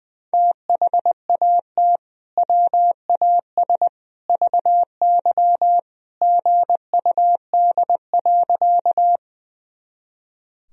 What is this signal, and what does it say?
Morse code, heard as a keyed tone, about 20 wpm, 700 Hz. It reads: THAT WAS VY GUD.
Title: Sentences from Top 200 Words and Abbreviations 20wpm